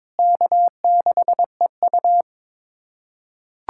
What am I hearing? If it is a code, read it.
Morse code, heard as a keyed tone, about 22 wpm, 700 Hz. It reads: K6EU
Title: K1USN SST QSO Practice - 22wpm